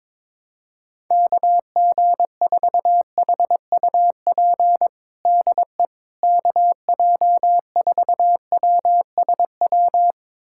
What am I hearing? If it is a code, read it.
Morse code, heard as a keyed tone, about 22 wpm, 700 Hz. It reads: KG4HUP DE KJ4WSW
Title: USA Callsign DE Callsign - 6 Characters - 22wpm